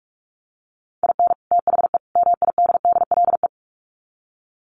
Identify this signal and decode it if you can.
Morse code, heard as a keyed tone, about 44 wpm, 700 Hz. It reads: IN THE MIDDLE
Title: Sets of 3 Words 45wpm